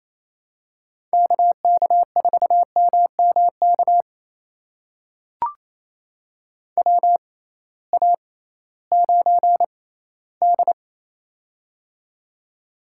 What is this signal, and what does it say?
Morse code, heard as a keyed tone, about 28 wpm, spaced as for 16 wpm, 700 Hz. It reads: KK4MMK WA9D
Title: USA Call Signs - 6x Character Spacing - 28wpm